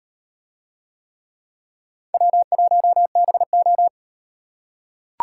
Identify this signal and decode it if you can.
Morse code, heard as a keyed tone, about 36 wpm, 700 Hz. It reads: W1BO